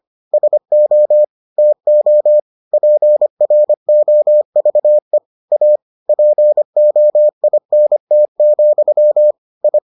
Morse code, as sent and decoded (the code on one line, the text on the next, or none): ... --- / - --- / .--. .-. --- ...- . / .- / .--. --- .. -. - --..-- / ..
SO TO PROVE A POINT, I